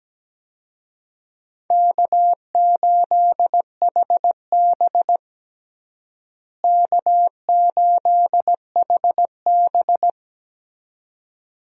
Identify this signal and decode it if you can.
Morse code, heard as a keyed tone, about 17 wpm, 700 Hz. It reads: K8HB K8HB
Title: USA Call Signs - 4 Characters - Repeated 2x - 17wpm